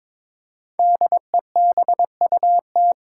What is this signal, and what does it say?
Morse code, heard as a keyed tone, about 23 wpm, 700 Hz. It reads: DEBUT
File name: 5 Letter Words - First 3 Letters Given - 22wpm